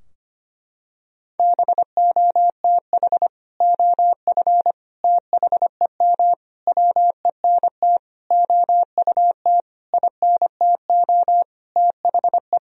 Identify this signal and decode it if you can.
Morse code, heard as a keyed tone, about 25 wpm, 700 Hz. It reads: BOTH OF THEM WENT OUT INTO THE
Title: Sentences from Top 1000 Words - Review Part 2 - 25wpm